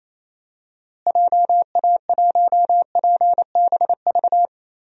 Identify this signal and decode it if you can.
Morse code, heard as a keyed tone, about 28 wpm, 700 Hz. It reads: JA1PBV
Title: DXCC-100 Call Signs - 6 Characters - Repeated 2x - 28wpm